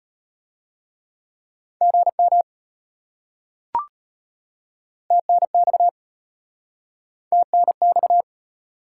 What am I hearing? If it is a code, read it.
Morse code, heard as a keyed tone, about 32 wpm, 700 Hz. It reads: GM TNX TNX